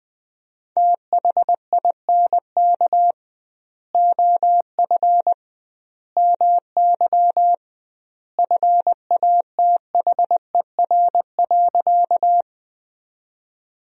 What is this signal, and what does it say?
Morse code, heard as a keyed tone, about 21 wpm, 700 Hz. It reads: THINK OF MY FATHER.
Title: 2x Word-Spacing - Sentences from Top 300 Words - 20wpm